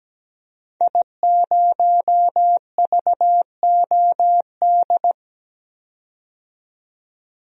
Morse code, heard as a keyed tone, about 17 wpm, 700 Hz. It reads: I0VOD
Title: DXCC-100 Call Signs - 5 Characters - 17wpm